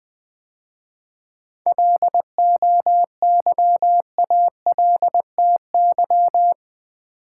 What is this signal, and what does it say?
Morse code, heard as a keyed tone, about 20 wpm, 700 Hz. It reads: LOYALTY
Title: 7 Letter Words - First 3 Letters Given - 20wpm